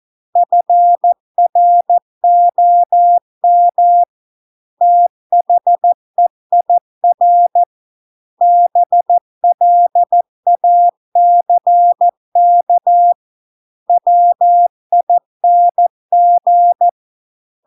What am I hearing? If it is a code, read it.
Morse code, heard as a keyed tone, about 14 wpm, 700 Hz. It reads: FROM THEIR BLACK WING